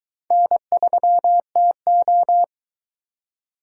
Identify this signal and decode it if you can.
Morse code, heard as a keyed tone, about 21 wpm, 700 Hz. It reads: N3TO